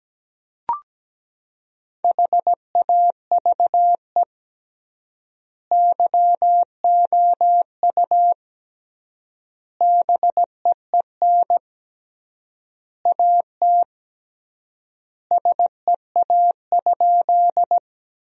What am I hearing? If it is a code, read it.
Morse code, heard as a keyed tone, about 17 wpm, 700 Hz. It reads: HAVE YOU BEEN AT SEA?
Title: 3x Word-Spacing - Sentences from Top 300 Words - 17wpm